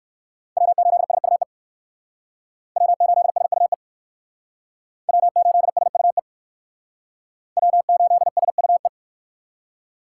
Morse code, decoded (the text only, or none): W8SUE W8SUE W8SUE W8SUE